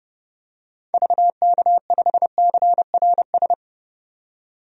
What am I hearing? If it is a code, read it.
Morse code, heard as a keyed tone, about 30 wpm, 700 Hz. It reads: VK5CRS
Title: DXCC-100 Call Signs - 6 Characters - 30wpm